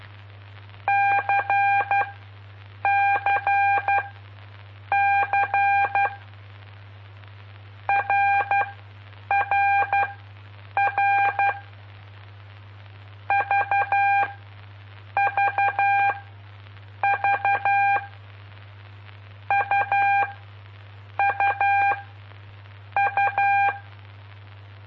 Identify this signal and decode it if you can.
Morse code, heard as a keyed tone, about 12 wpm, spaced as for 7 wpm, 800 Hz. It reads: CCC RRR VVV UUU
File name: pp5bar / aulas de telegrafia / Aula-05